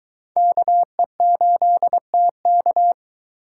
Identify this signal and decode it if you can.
Morse code, heard as a keyed tone, about 21 wpm, 700 Hz. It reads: KE8TK